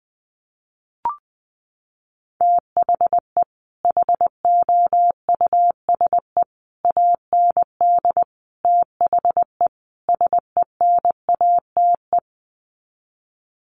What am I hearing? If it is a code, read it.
Morse code, heard as a keyed tone, about 20 wpm, 700 Hz. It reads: THE HOUSE AND THE SENATE
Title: Sets of 5 Words Encore2 20wpm